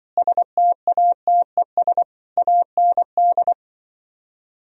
Morse code, heard as a keyed tone, about 24 wpm, 700 Hz. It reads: STATES AND
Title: Speed-Racing - Sets of 5 Words 20wpm